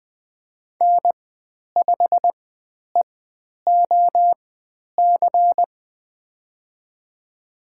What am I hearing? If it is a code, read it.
Morse code, heard as a keyed tone, about 20 wpm, spaced as for 10 wpm, 700 Hz. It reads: N5EOC
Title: 109 Lesson 18g - Letter m - Call Signs 20wpm